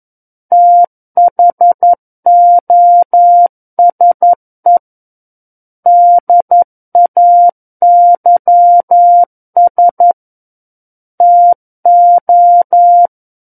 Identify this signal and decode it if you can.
Morse code, heard as a keyed tone, about 11 wpm, 700 Hz. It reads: THOSE DAYS TO
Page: War of the Worlds - 02-Chapter 2 - 11 WPM